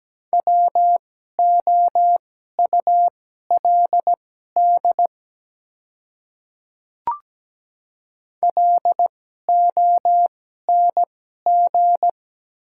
Morse code, heard as a keyed tone, about 17 wpm, spaced as for 12 wpm, 690 Hz. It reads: WOULD LONG
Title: 2x Character Spacing - Top 100 Words - 17wpm